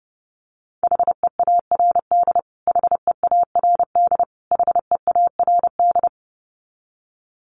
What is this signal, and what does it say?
Morse code, heard as a keyed tone, about 30 wpm, 700 Hz. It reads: HEARD HEARD HEARD
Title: Top 500 Words Repeated 30wpm